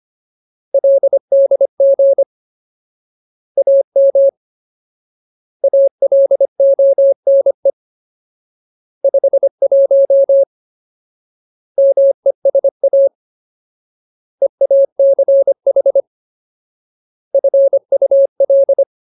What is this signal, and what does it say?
Morse code, heard as a keyed tone, about 25 wpm, 550 Hz. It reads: LDG AM ALONE 51 MESA EACH FUL
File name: Ultimate Word List / 25WPM Ultimate